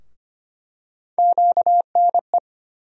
Morse code, heard as a keyed tone, about 25 wpm, 700 Hz. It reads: QNE
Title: USA Call Sign Postfixes - 3 Letters 25wpm